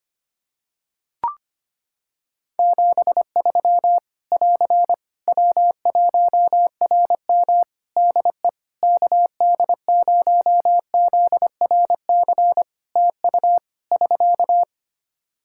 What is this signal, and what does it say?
Morse code, heard as a keyed tone, about 25 wpm, 700 Hz. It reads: 73 <AR> W1RM DE KD0ZRC TU <SK>
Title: Practice QSO - Exchange 4 - Ending - 25wpm